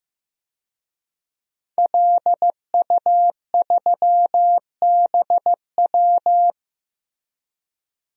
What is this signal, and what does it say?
Morse code, heard as a keyed tone, about 15 wpm, 700 Hz. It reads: LU3BW
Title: DXCC-100 Call Signs - 5 Characters - Repeated 3x - 15wpm